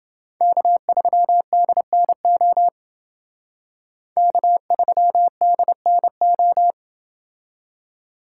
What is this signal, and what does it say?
Morse code, heard as a keyed tone, about 28 wpm, 700 Hz. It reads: K3DNO K3DNO